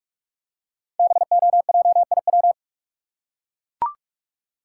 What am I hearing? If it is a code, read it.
Morse code, heard as a keyed tone, about 44 wpm, 700 Hz. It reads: BOJIW
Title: Code Groups - 5 Letters - 45wpm